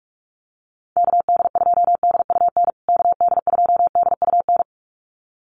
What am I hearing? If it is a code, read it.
Morse code, heard as a keyed tone, about 44 wpm, 700 Hz. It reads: KD2DUN KD2DUN
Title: USA Call Signs Repeated 2x 45wpm